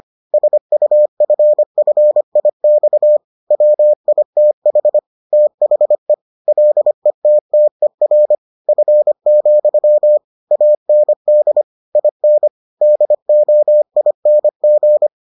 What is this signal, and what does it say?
Morse code, heard as a keyed tone, about 25 wpm, 600 Hz. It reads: SUFFIX WITH THE LETTER F, AND IN DOING